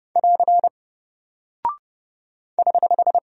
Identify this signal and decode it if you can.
Morse code, heard as a keyed tone, about 30 wpm, 700 Hz. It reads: <AR> <HH>
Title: Single Character (Letter:Number:Punctuation:Prosign) 30wpm